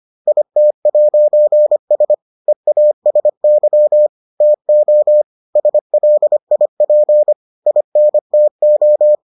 Morse code, heard as a keyed tone, about 25 wpm, 600 Hz. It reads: IT'S EASY TO SLIP INTO